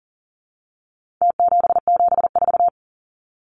Morse code, heard as a keyed tone, about 40 wpm, 700 Hz. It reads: T774
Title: Numbers - Contest Style - 4-Digit 40wpm